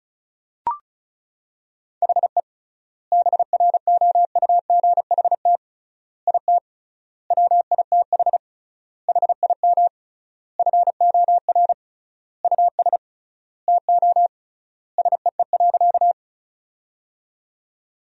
Morse code, heard as a keyed tone, about 36 wpm, 700 Hz. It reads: HE BROUGHT IT WITH HIM FOR US TO SEE.
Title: 3x Word-Spacing - Sentences from Top 500 Words - 35wpm